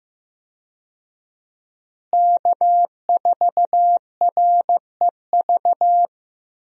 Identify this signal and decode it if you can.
Morse code, heard as a keyed tone, about 15 wpm, 700 Hz. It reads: K4REV
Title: USA Call Signs - 5 Characters - Repeated 2x - 15wpm